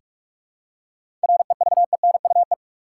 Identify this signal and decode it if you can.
Morse code, heard as a keyed tone, about 46 wpm, 700 Hz. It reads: REVENUE
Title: Seven Letter Words - 45wpm